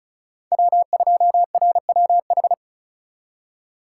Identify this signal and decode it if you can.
Morse code, heard as a keyed tone, about 36 wpm, 700 Hz. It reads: W2RWH
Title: USA Call Signs - 5 Characters - 2x Character Spacing - 35wpm